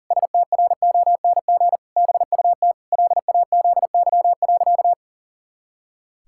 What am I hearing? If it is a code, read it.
Morse code, heard as a keyed tone, about 37 wpm, 700 Hz. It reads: STRONG BUT LAZY.
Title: Pangram Challenge v2 - 40wpm